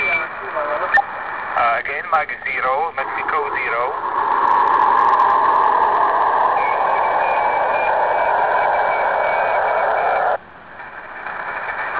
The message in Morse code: none